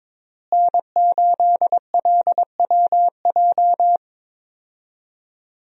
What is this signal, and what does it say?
Morse code, heard as a keyed tone, about 22 wpm, 700 Hz. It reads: N8LWJ
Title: USA Call Signs - 5 Characters - Repeated 3x - 22wpm